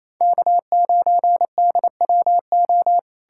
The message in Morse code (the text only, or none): K9DWO